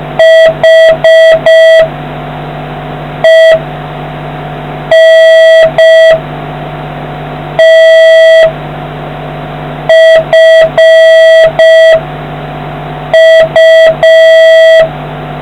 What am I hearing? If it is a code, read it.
Morse code, heard as a keyed tone, about 5 wpm, 1950 Hz, with noise. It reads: H E N T F U